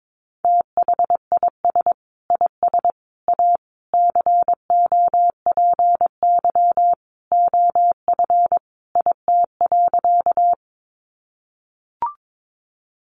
Morse code, heard as a keyed tone, about 22 wpm, 700 Hz. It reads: THIS IS A COPY OF IT.
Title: Sentences from Top 1400 Words 22wpm